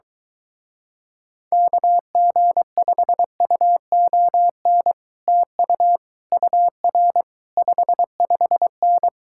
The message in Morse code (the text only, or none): KG5UON TU UR 55N